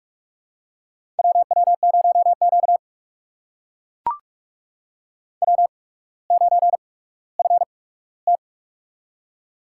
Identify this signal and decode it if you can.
Morse code, heard as a keyed tone, about 44 wpm, spaced as for 19 wpm, 700 Hz. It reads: WW0Q W9FT
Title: USA Call Signs - 8x Character Spacing - 45wpm